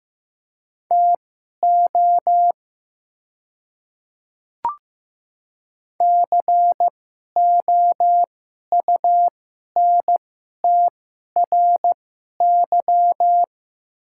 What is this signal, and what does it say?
Morse code, heard as a keyed tone, about 15 wpm, spaced as for 11 wpm, 700 Hz. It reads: TO COUNTRY
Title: 2x Character Spacing - Top 500 Words - 15wpm